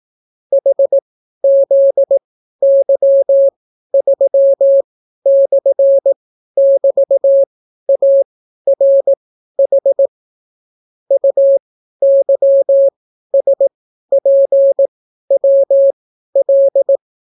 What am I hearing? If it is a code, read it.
Morse code, heard as a keyed tone, about 18 wpm, spaced as for 12 wpm, 550 Hz. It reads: HZY3/<BT>ARH UYSPWL